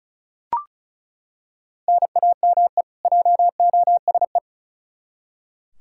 Morse code, heard as a keyed tone, about 36 wpm, 700 Hz. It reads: NAME JOSE
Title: Practice QSO - Exchange 1 - Beginning - 35wpm